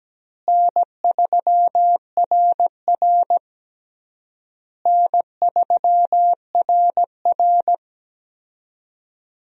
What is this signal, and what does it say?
Morse code, heard as a keyed tone, about 17 wpm, 700 Hz. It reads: N3RR N3RR
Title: USA Call Signs - 4 Characters - Repeated 3x - 17wpm